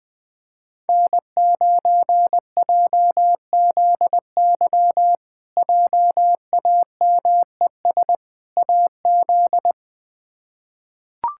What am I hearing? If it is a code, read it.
Morse code, heard as a keyed tone, about 20 wpm, 700 Hz. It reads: N9JZY JAMES AZ
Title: K1USN SST QSO Practice - 20wpm